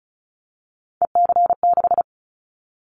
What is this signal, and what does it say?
Morse code, heard as a keyed tone, about 34 wpm, 700 Hz. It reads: EC6
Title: Callsign Prefixes for Top 100 DXCC - 35wpm